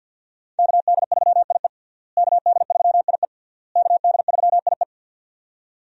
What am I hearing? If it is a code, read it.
Morse code, heard as a keyed tone, about 53 wpm, 700 Hz. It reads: KD3IE KD3IE KD3IE
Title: USA Call Signs - 5 Characters - Repeated 3x - 50wpm